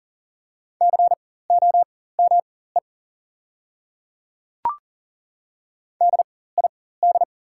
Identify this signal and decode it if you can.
Morse code, heard as a keyed tone, about 40 wpm, spaced as for 19 wpm, 700 Hz. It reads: COME DID
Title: 4x Character Spacing - Top 100 Words - 40wpm